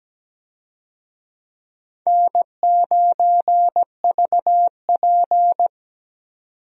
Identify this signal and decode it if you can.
Morse code, heard as a keyed tone, about 17 wpm, 700 Hz. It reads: N9VP
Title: USA Call Signs - 4 Characters - 17wpm